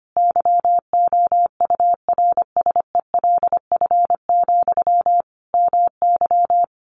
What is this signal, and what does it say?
Morse code, heard as a keyed tone, about 25 wpm, 700 Hz. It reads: YOURSELF, MY
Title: Sentences from Top 1100 Words 25wpm